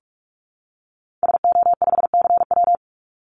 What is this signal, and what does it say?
Morse code, heard as a keyed tone, about 44 wpm, 700 Hz. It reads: SO5CW
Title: Call Signs Repeated 2x - 45wpm